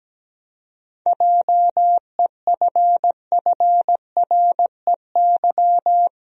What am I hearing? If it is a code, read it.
Morse code, heard as a keyed tone, about 17 wpm, 700 Hz. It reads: JEFFREY
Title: USA Common Names - 17wpm